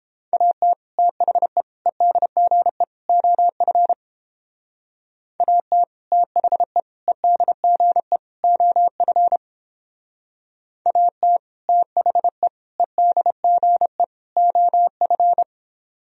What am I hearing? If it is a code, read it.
Morse code, heard as a keyed tone, about 29 wpm, 700 Hz. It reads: AT THE EDGE OF AT THE EDGE OF AT THE EDGE OF